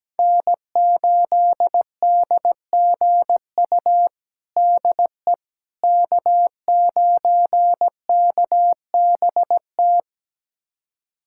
Morse code, heard as a keyed tone, about 17 wpm, 700 Hz. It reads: N8DGU DE K9KBT
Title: USA Callsign DE Callsign - 5 Characters - 17wpm